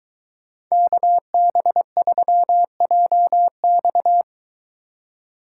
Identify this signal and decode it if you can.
Morse code, heard as a keyed tone, about 23 wpm, 700 Hz. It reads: KB3JX